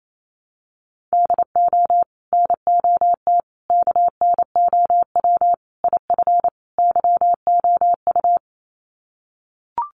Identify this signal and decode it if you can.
Morse code, heard as a keyed tone, about 28 wpm, 700 Hz. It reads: DO NOT KNOW IF YOU
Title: Sets of 5 Words 28wpm